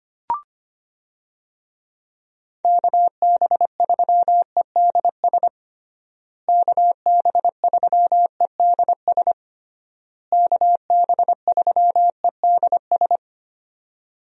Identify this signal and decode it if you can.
Morse code, heard as a keyed tone, about 25 wpm, 700 Hz. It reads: KB3EDS KB3EDS KB3EDS
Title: USA Call Signs - 6 Characters - Repeated 3x - 25wpm